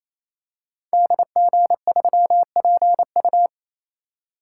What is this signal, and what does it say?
Morse code, heard as a keyed tone, about 28 wpm, 700 Hz. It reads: DG3PU
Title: Code Groups - 5 Letter:Numbers - 28wpm